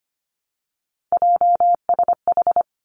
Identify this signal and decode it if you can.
Morse code, heard as a keyed tone, about 25 wpm, 700 Hz. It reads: JSH
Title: Three Challenging Letter-Number Combinations 25wpm